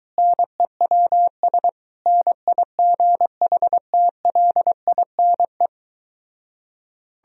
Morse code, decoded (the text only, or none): NEWS NIGHTLINE